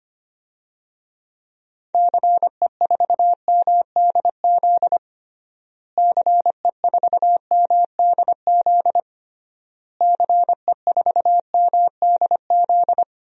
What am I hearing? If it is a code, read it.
Morse code, heard as a keyed tone, about 25 wpm, 700 Hz. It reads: CE4MDZ CE4MDZ CE4MDZ
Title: DXCC-100 Call Signs - 6 Characters - Repeated 3x - 25wpm